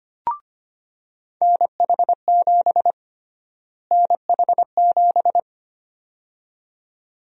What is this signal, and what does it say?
Morse code, heard as a keyed tone, about 25 wpm, 700 Hz. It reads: NH7 NH7
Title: USA Call Sign Prefixes - Repeated 2x - 25wpm